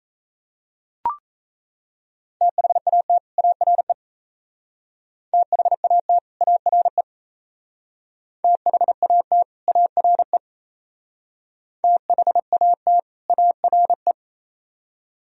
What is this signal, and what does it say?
Morse code, heard as a keyed tone, about 35 wpm, 700 Hz. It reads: THAT ARE THAT ARE THAT ARE THAT ARE